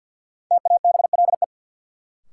Morse code, heard as a keyed tone, about 50 wpm, 700 Hz. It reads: TABLE
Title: Sentences from Top 400 Words - New Word Drill - 50wpm